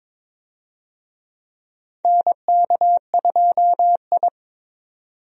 Morse code, heard as a keyed tone, about 22 wpm, 700 Hz. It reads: NK2I
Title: USA Call Signs - 4 Characters - 22wpm